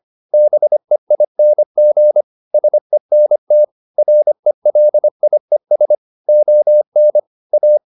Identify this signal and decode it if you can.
Morse code, heard as a keyed tone, about 25 wpm, 600 Hz. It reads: BEING SENT RELIES ON A